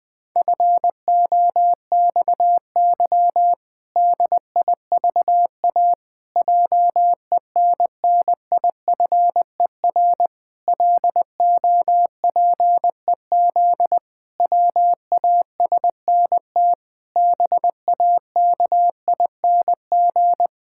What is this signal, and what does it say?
Morse code, heard as a keyed tone, about 20 wpm, 700 Hz. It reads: FOXY DIVA JENNIFER LOPEZ WASNT BAKING
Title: Pangram Challenge v2 - 20wpm